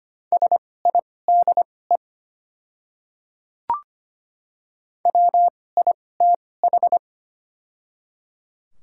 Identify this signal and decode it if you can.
Morse code, heard as a keyed tone, about 25 wpm, spaced as for 18 wpm, 700 Hz. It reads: SIDE WITH
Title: 2x Character Spacing - Top 100 Words - 25wpm